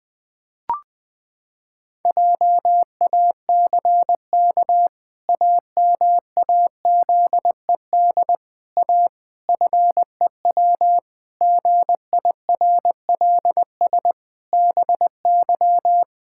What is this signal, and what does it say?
Morse code, heard as a keyed tone, about 20 wpm, 700 Hz. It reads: JACK AMAZED A FEW GIRLS BY
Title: Pangram Challenge v2 - 20wpm